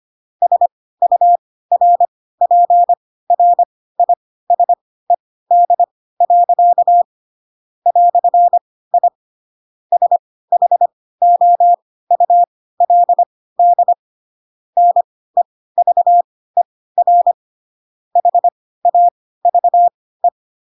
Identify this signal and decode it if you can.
Morse code, heard as a keyed tone, about 25 wpm, spaced as for 16 wpm, 700 Hz. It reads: SURPRISED. "I SHOULD NEVER HAVE
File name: Pinocchio / Pinocchio ch7 10 25 ik1hge